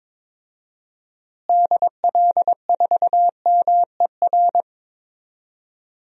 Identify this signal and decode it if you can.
Morse code, heard as a keyed tone, about 22 wpm, 700 Hz. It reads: DL4MER
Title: DXCC-100 Call Signs - 6 Characters - Repeated 2x - 22wpm